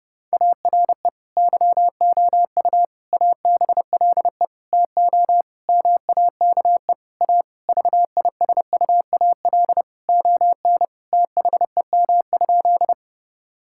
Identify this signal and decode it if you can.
Morse code, heard as a keyed tone, about 32 wpm, 700 Hz. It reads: ARE YOU ABLE TO MAKE A VISUAL ON THEM?
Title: Sentences from Top 1000 Words - Review Part 6 - 30wpm